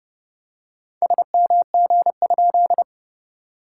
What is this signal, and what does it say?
Morse code, heard as a keyed tone, about 30 wpm, 700 Hz. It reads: SMG?
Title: SOTA QSO Practice 30wpm